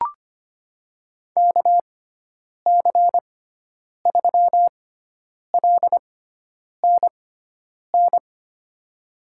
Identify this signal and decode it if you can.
Morse code, heard as a keyed tone, about 25 wpm, spaced as for 9 wpm, 700 Hz. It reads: KC3LNN
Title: USA Call Signs - 6 Characters - 6x Character Spacing - 25wpm